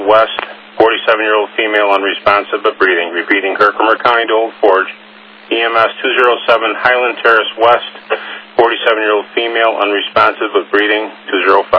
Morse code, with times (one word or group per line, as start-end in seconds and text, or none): none